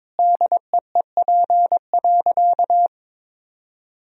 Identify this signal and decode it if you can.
Morse code, heard as a keyed tone, about 21 wpm, 700 Hz. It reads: DEEP.
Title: Sentences from Top 1000 Words - Review Part 3 - 22wpm